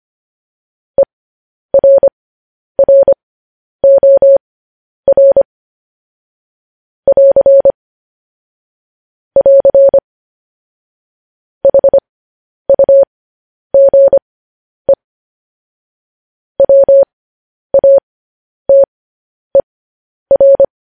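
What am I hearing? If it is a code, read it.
Morse code, heard as a keyed tone, about 25 wpm, spaced as for 10 wpm, 550 Hz. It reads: ERROR <AR> <AR> HUGE WATER
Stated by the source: Morse code news headlines sent at 10 words per minute.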